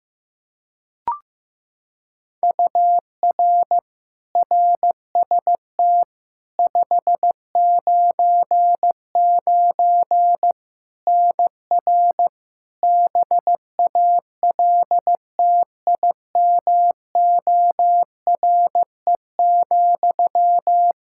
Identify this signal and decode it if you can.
Morse code, heard as a keyed tone, about 15 wpm, 700 Hz. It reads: UR RST 599 NR BALTIMORE,
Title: Practice QSO - Exchange 1 - Beginning - 15wpm